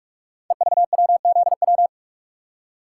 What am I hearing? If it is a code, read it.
Morse code, heard as a keyed tone, about 46 wpm, 700 Hz. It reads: EVWZW
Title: Code Groups - 5 Letter:Numbers - 45wpm